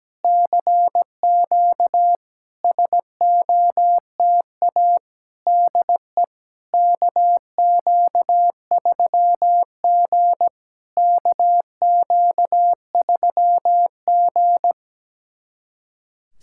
Morse code, heard as a keyed tone, about 17 wpm, 700 Hz. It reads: CQ SOTA DE KQ3G KQ3G
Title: SOTA QSO Practice 17wpm